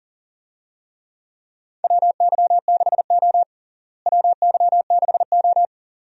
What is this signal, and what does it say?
Morse code, heard as a keyed tone, about 40 wpm, 700 Hz. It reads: WY6O WY6O
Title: USA Call Signs - 4 Characters - Repeated 2x - 40wpm